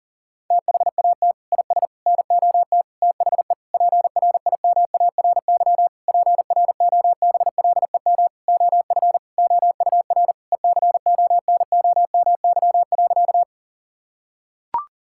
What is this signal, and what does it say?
Morse code, heard as a keyed tone, about 40 wpm, 700 Hz. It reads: THAT IS NOT THE PRIMARY PROBLEM OF OUR ECONOMY.
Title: Sentences from Top 1000 Words - Review Part 5 - 40wpm